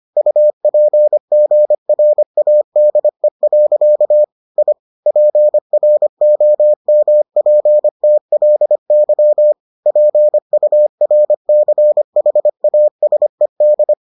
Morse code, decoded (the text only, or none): UPGRADE. I PROMPTLY PURCHASED